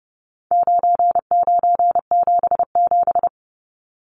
Morse code, heard as a keyed tone, about 30 wpm, 700 Hz. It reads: 9977
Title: Numbers - 4-Digit 30wpm